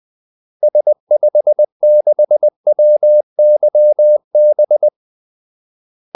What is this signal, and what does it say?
Morse code, heard as a keyed tone, about 20 wpm, 600 Hz. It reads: S56WYB
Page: Kaldesignaler 20 wpm